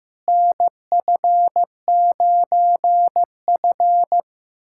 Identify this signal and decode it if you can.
Morse code, heard as a keyed tone, about 15 wpm, 700 Hz. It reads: NF9F
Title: USA Callsign DE Callsign - 4 Characters - 15wpm